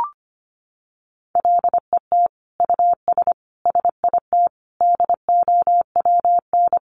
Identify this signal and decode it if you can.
Morse code, heard as a keyed tone, about 25 wpm, 700 Hz. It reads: LET US SIT DOWN
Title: Sentences from Top 1400 Words 25wpm